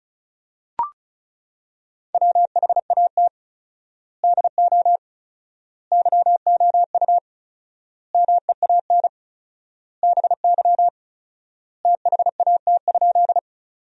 Morse code, heard as a keyed tone, about 36 wpm, 700 Hz. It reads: WHAT DO YOU MEAN BY THAT?
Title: 4x Word-Spacing - Sentences from Top 200 Words - 35wpm